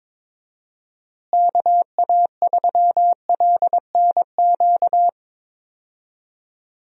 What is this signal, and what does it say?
Morse code, heard as a keyed tone, about 22 wpm, 700 Hz. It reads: KA3LNQ
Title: USA Call Signs - 6 Characters - Repeated 2x - 22wpm